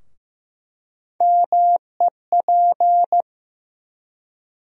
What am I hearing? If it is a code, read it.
Morse code, heard as a keyed tone, about 15 wpm, 700 Hz. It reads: MEP
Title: USA Call Sign Postfixes - 3 Letters 15wpm